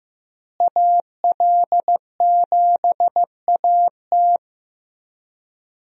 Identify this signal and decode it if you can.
Morse code, heard as a keyed tone, about 15 wpm, 700 Hz. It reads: AL7AT
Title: USA Call Signs - 5 Characters - Repeated 3x - 15wpm